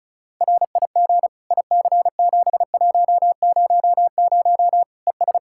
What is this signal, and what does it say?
Morse code, heard as a keyed tone, about 36 wpm, 700 Hz. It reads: RIG IC7100 ES
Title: Practice QSO - Exchange 2 - Info - 35wpm